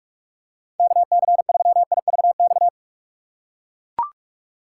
Morse code, heard as a keyed tone, about 44 wpm, 700 Hz. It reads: KC3IVX
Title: USA Call Signs - 6 Characters - 8x Character Spacing - 45wpm